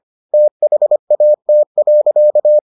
Morse code, heard as a keyed tone, about 25 wpm, 600 Hz. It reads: THAT.